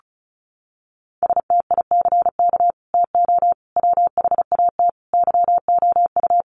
Morse code, heard as a keyed tone, about 34 wpm, 700 Hz. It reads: STICK TO WHAT YOU
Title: Sentences from Top 1200 Words 35wpm